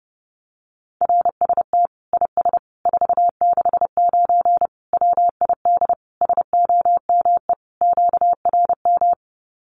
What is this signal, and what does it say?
Morse code, heard as a keyed tone, about 28 wpm, 700 Hz. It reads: RST IS 469 WID SOME QRM
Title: Common QSO Phrases 30wpm